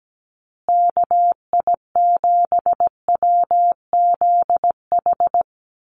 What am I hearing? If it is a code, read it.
Morse code, heard as a keyed tone, about 17 wpm, 700 Hz. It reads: KI7WZH
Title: USA Call Signs Repeated 2x 17wpm